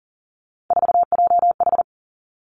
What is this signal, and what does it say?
Morse code, heard as a keyed tone, about 42 wpm, 700 Hz. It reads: 4JH
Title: Three Challenging Letter-Number Combinations 40wpm